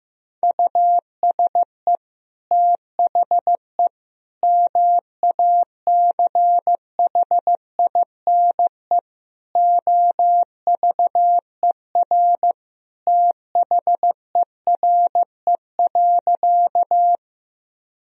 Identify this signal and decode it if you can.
Morse code, heard as a keyed tone, about 15 wpm, 700 Hz. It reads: USE THE MACHINE OVER THERE.
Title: Sentences from Top 1000 Words - Review Part 2 - 15wpm